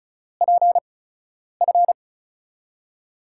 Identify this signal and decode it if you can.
Morse code, heard as a keyed tone, about 36 wpm, spaced as for 10 wpm, 700 Hz. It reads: PF
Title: USA Call Signs - 4 Characters - 8x Character Spacing - 35wpm